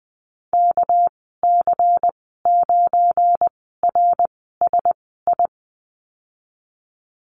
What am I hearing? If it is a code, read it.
Morse code, heard as a keyed tone, about 20 wpm, spaced as for 15 wpm, 700 Hz. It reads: KC9RSI
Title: 238 Lesson 38g - Letter q - Call Signs 20wpm